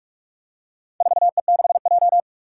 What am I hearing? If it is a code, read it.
Morse code, heard as a keyed tone, about 46 wpm, 700 Hz. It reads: 4E6J
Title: Four Challenging Letter-Number Combinations 45wpm